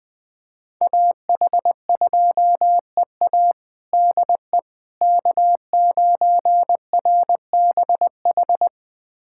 Morse code, heard as a keyed tone, about 21 wpm, 700 Hz. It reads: AH2EA DE K9RBH
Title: USA Callsign DE Callsign - 5 Characters - 20wpm